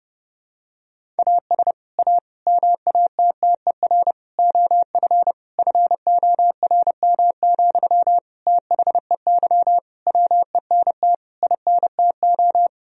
Morse code, heard as a keyed tone, about 30 wpm, 700 Hz. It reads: AS A MATTER OF FORM, THEY WENT INTO
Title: Sentences from Top 1000 Words - Review Part 1 - 30wpm